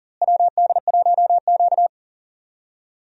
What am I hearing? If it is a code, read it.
Morse code, heard as a keyed tone, about 40 wpm, 700 Hz. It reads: WD1Q
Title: USA Call Signs - 4 Characters - Repeated 2x - 40wpm